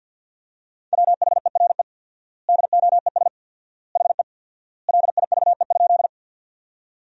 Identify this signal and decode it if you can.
Morse code, heard as a keyed tone, about 47 wpm, 700 Hz. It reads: WHERE DOES HE LIVE?
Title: 4x Word-Spacing - Sentences from Top 200 Words - 50wpm